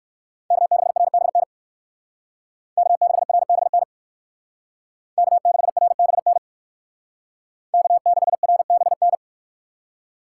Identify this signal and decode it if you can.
Morse code, heard as a keyed tone, about 59 wpm, 700 Hz. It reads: K6RBN K6RBN K6RBN K6RBN